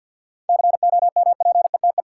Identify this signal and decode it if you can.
Morse code, heard as a keyed tone, about 48 wpm, 700 Hz. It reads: COMPETE
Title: Seven Letter Words - 50wpm